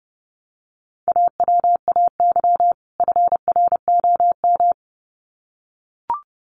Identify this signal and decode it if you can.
Morse code, heard as a keyed tone, about 30 wpm, 700 Hz. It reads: AWAY FROM
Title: Sets of 2 Words 30wpm